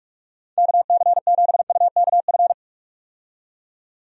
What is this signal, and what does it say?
Morse code, heard as a keyed tone, about 46 wpm, 700 Hz. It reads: KC7UKF
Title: USA Call Signs - 6 Characters - 45wpm